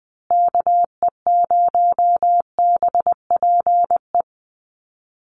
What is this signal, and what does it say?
Morse code, heard as a keyed tone, about 20 wpm, 700 Hz. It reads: KE0BPE
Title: USA Call Signs Repeated 3x 20wpm